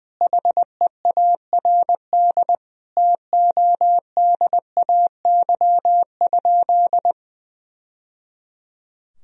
Morse code, heard as a keyed tone, about 20 wpm, 700 Hz. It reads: HEARD TODAY?
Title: Sentences from Top 1000 Words - Review Part 4 - 20wpm